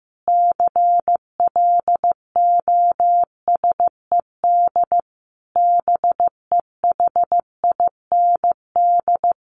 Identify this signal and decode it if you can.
Morse code, heard as a keyed tone, about 15 wpm, 700 Hz. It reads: CLOSED BEHIND
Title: Sentences from Top 1400 Words 15wpm